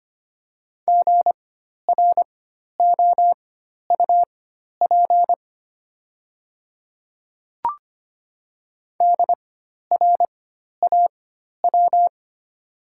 Morse code, heard as a keyed tone, about 25 wpm, spaced as for 12 wpm, 700 Hz. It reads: GROUP DRAW
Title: 4x Character Spacing - Top 300 Words - 25wpm